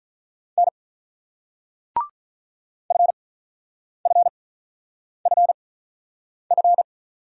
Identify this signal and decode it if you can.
Morse code, heard as a keyed tone, about 47 wpm, spaced as for 9 wpm, 690 Hz. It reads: N FFFF